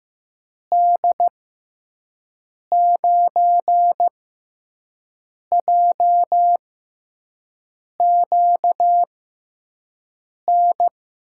Morse code, heard as a keyed tone, about 15 wpm, spaced as for 5 wpm, 700 Hz. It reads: D9JQN
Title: USA Call Signs - 6 Characters - 6x Character Spacing - 15wpm